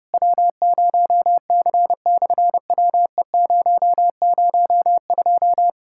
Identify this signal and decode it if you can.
Morse code, heard as a keyed tone, about 30 wpm, 700 Hz. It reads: W0C/WE002
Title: SOTA QSO Practice 30wpm